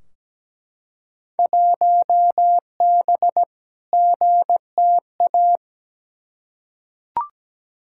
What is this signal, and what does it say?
Morse code, heard as a keyed tone, about 17 wpm, 700 Hz. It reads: 1B GTA
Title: Field Day Exchanges Encore - 17wpm